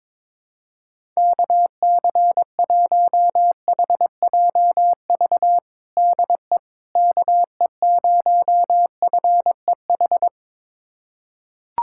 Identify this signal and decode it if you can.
Morse code, heard as a keyed tone, about 22 wpm, 700 Hz. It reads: KC1HJV DE KE0FEH
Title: USA Callsign DE Callsign 22wpm